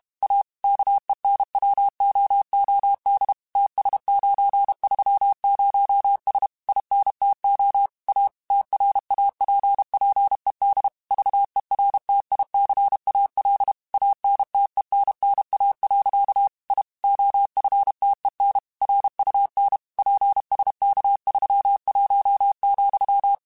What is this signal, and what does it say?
Morse code, heard as a keyed tone, about 32 wpm, 800 Hz. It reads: A KENWOOD TS930S INTO A TRAPPED VERTICAL ANTENNA. I OFTEN RUN PSK31,
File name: QSOs / QSOs-mp3s / WA7QOX 2@30